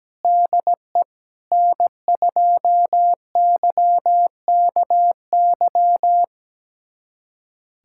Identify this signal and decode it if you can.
Morse code, heard as a keyed tone, about 17 wpm, 700 Hz. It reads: DE N2YKY
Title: USA Callsign DE Callsign - 5 Characters - 17wpm